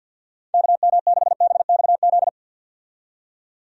Morse code, heard as a keyed tone, about 50 wpm, 700 Hz. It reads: KM6BXZ
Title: USA Call Signs - 6 Characters - 50wpm